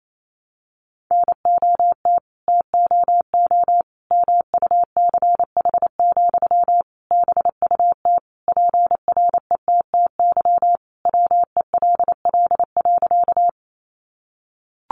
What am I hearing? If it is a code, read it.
Morse code, heard as a keyed tone, about 28 wpm, 700 Hz. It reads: NOT TOO MUCH, BUT PRETTY WELL.
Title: Sentences from Top 1000 Words 28wpm